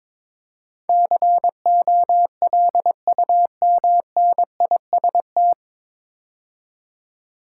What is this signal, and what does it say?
Morse code, heard as a keyed tone, about 22 wpm, 700 Hz. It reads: COLUMNIST
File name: 9 Letter Words - First 3 Letters Given - 22wpm